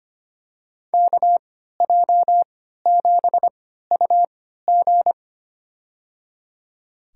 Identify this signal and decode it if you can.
Morse code, heard as a keyed tone, about 25 wpm, spaced as for 14 wpm, 700 Hz. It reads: KJ7UG
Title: USA Call Signs - 3x Character Spacing - 25wpm